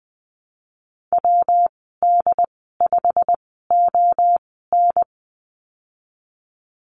Morse code, heard as a keyed tone, about 20 wpm, spaced as for 15 wpm, 700 Hz. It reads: WD5ON
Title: 167 Lesson 27g - Number 7 - Call Signs 20wpm